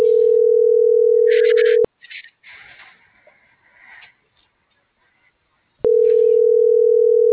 Ringback tone (North America), 440 + 480 Hz.